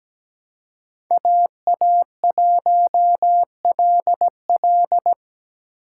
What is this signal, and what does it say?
Morse code, heard as a keyed tone, about 17 wpm, 700 Hz. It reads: AA1LL
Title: USA Call Signs - 5 Characters - 2x Character Spacing - 17wpm